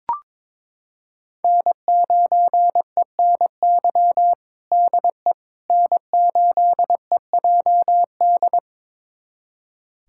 Morse code, heard as a keyed tone, about 22 wpm, 700 Hz. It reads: N9ENY DE N8EJD
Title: USA Callsign DE Callsign - 5 Characters - 22wpm